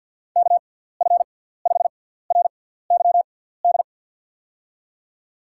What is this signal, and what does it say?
Morse code, heard as a keyed tone, about 46 wpm, spaced as for 17 wpm, 700 Hz. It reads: KF5RYD
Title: USA Call Signs - 6x Character Spacing - 50wpm